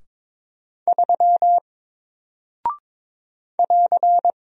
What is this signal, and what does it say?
Morse code, heard as a keyed tone, about 22 wpm, 700 Hz. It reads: 3 <AR>
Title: Single Character (Letter:Number:Punctuation:Prosign) Alt 22wpm